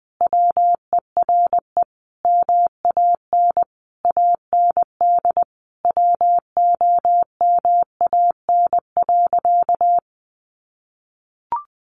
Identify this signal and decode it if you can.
Morse code, heard as a keyed tone, about 20 wpm, 700 Hz. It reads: WERE MAN AND WOMAN.
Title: Sentences from Top 1100 Words 20wpm